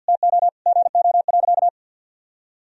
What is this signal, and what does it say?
Morse code, heard as a keyed tone, about 50 wpm, 700 Hz. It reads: TO GO.
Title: Sentences from Top 1000 Words - Review Part 4 - 50wpm